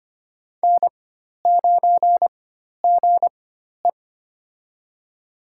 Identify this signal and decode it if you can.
Morse code, heard as a keyed tone, about 25 wpm, spaced as for 12 wpm, 700 Hz. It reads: N9GE
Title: USA Call Signs - 4 Characters - 4x Character Spacing - 25wpm